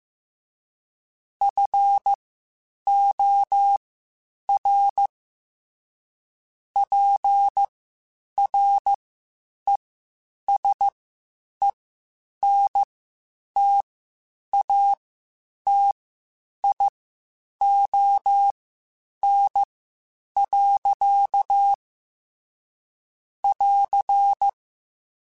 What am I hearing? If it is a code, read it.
Morse code, heard as a keyed tone, about 15 wpm, spaced as for 8 wpm, 800 Hz. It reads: FOR PRESENTATION. <AR>
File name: Quotes Spaced / Quotes Spaced-mp3s / Persperation sp @13